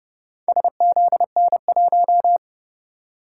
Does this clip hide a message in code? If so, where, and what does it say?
Yes: Morse code from 0.5 to 2.4 s, heard as a keyed tone, about 30 wpm, 700 Hz. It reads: SZN1